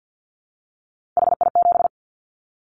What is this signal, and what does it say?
Morse code, heard as a keyed tone, about 50 wpm, 700 Hz. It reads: HI7
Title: Callsign Prefixes for Top 100 DXCC - 50wpm